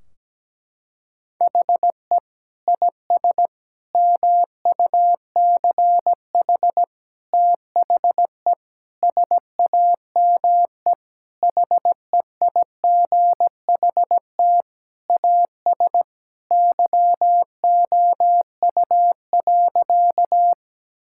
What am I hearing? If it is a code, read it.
Morse code, heard as a keyed tone, about 17 wpm, 700 Hz. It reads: HE IS MUCH THE SAME HEIGHT AS YOU.
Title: Sentences from Top 1600 Words 17wpm